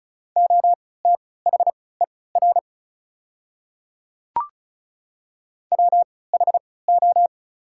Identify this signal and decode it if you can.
Morse code, heard as a keyed tone, about 36 wpm, spaced as for 20 wpm, 690 Hz. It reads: OTHER WHO
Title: 3x Character Spacing - Top 100 Words - 35wpm